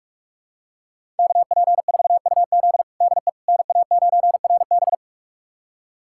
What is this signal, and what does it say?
Morse code, heard as a keyed tone, about 41 wpm, 700 Hz. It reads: KP4UZ DE NA9RB
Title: USA Callsign DE Callsign - 5 Characters - 45wpm